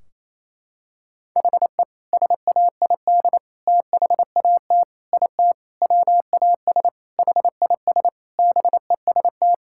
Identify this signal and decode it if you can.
Morse code, heard as a keyed tone, about 28 wpm, 700 Hz. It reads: HE SAID THAT IT WAS HIS BEST
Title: Sentences from Top 1600 Words 28wpm